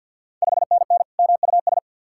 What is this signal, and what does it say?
Morse code, heard as a keyed tone, about 50 wpm, 690 Hz. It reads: 5NN MUS
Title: Ontario QSO Party Exchange 50wpm